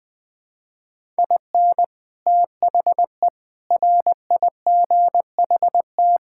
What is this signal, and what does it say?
Morse code, heard as a keyed tone, about 20 wpm, 700 Hz. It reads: IN THE RIGHT
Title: Sets of 8 Words - 20wpm